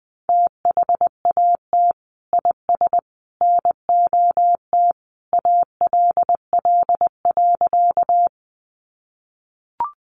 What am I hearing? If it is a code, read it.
Morse code, heard as a keyed tone, about 20 wpm, 700 Hz. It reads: THAT IS NOT ALL.
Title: Sentences from Top 600 Words 20wpm